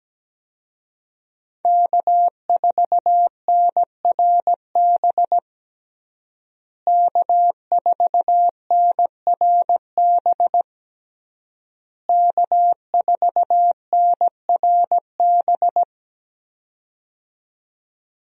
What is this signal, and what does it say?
Morse code, heard as a keyed tone, about 17 wpm, 700 Hz. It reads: K4NRB K4NRB K4NRB
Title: USA Call Signs - 5 Characters - Repeated 3x - 17wpm